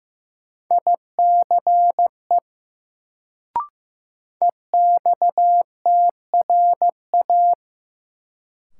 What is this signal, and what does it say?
Morse code, heard as a keyed tone, about 15 wpm, 700 Hz. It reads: ICE EXTRA
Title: Sentences from Top 1000 Words - New Word Drill - 15wpm